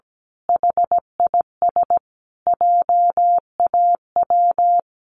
Morse code, heard as a keyed tone, about 17 wpm, 700 Hz. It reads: HIS JAW
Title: A Sound Of Thunder 17wpm